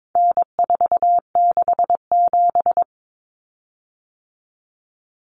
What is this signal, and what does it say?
Morse code, heard as a keyed tone, about 22 wpm, 700 Hz. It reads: N467
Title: Common Cut Numbers - 4-Digit 22wpm